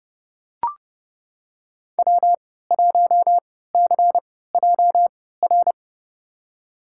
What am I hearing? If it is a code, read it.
Morse code, heard as a keyed tone, about 30 wpm, spaced as for 17 wpm, 700 Hz. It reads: W1CJR
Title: USA Call Signs - 5 Characters - 3x Character Spacing - 30wpm